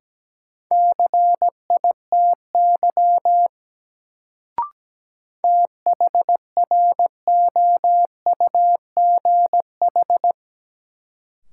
Morse code, heard as a keyed tone, about 17 wpm, 700 Hz. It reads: CITY THROUGH
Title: Sets of 2 Words - New Word Drill - 17wpm